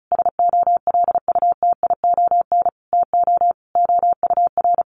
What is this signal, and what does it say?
Morse code, heard as a keyed tone, about 34 wpm, 700 Hz. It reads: SOLUTION TO OUR
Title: Sentences from Top 1300 Words 35wpm